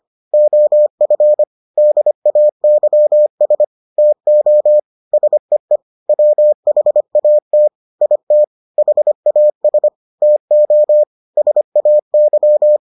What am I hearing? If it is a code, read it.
Morse code, heard as a keyed tone, about 25 wpm, 600 Hz. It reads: OF DAYS TO SEE WHAT IT HAS TO SAY